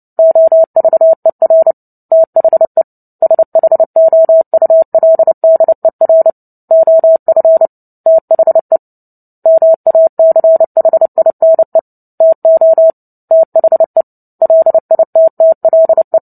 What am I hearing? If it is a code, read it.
Morse code, heard as a keyed tone, about 30 wpm, 650 Hz. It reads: OVER THE SHOULDER OF THE MACHINE TO THE LITTLE
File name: Book Two / 03-Chapter 3 - 29 WPM